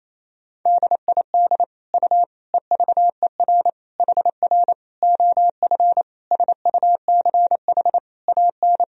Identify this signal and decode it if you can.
Morse code, heard as a keyed tone, about 27 wpm, 700 Hz. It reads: DID U EVER HR OF SUCH AN
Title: Sentences from Top 400 Words and Abbreviations 28wpm